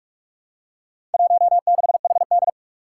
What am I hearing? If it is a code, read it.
Morse code, heard as a keyed tone, about 46 wpm, 700 Hz. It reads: 16HD
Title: Four Challenging Letter-Number Combinations 45wpm